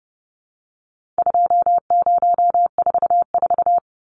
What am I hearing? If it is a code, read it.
Morse code, heard as a keyed tone, about 30 wpm, 700 Hz. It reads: 2044
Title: Numbers - 4-Digit 30wpm